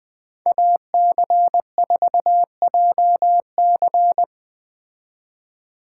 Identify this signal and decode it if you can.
Morse code, heard as a keyed tone, about 20 wpm, 700 Hz. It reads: AC4JC